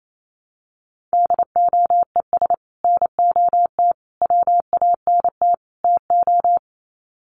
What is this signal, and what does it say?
Morse code, heard as a keyed tone, about 28 wpm, 700 Hz. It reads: DOES NOT WANT TO
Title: Sets of 4 Words 28wpm